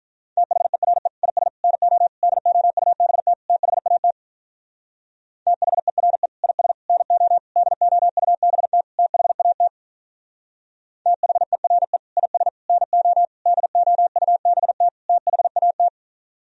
Morse code, heard as a keyed tone, about 42 wpm, 700 Hz. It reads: THERE IS NO DOUBT THAT THERE IS NO DOUBT THAT THERE IS NO DOUBT THAT
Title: Speed-Racing - Sets of 5 Words 35wpm